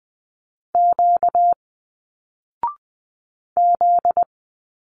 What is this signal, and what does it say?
Morse code, heard as a keyed tone, about 20 wpm, 700 Hz. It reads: Q Z